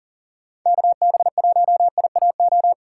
Morse code, heard as a keyed tone, about 50 wpm, 700 Hz. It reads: KB1IAO
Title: Speed Racing - USA 2x3 Call Signs 40wpm